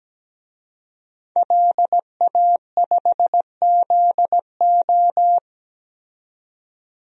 Morse code, heard as a keyed tone, about 17 wpm, 700 Hz. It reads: LA5ZO
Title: DXCC-100 Call Signs - 5 Characters - 17wpm